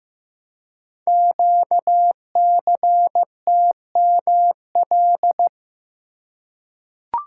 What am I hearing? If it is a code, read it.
Morse code, heard as a keyed tone, about 15 wpm, 700 Hz. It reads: QCTML
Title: Code Groups - 5 Letters - 15wpm